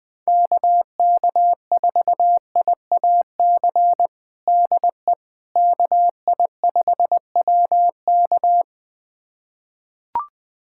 Morse code, heard as a keyed tone, about 20 wpm, 700 Hz. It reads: KK4IAC DE KI5WK
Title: USA Callsign DE Callsign 20wpm